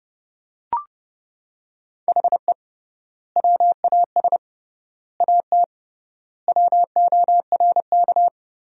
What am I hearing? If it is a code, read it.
Morse code, heard as a keyed tone, about 31 wpm, 700 Hz. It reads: HE WAS AT WORK
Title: 3x Word-Spacing - Sentences from Top 200 Words - 30wpm